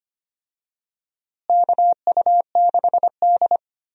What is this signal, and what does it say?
Morse code, heard as a keyed tone, about 25 wpm, 700 Hz. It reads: KU6D
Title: USA Call Signs - 4 Characters - Repeated 3x - 25wpm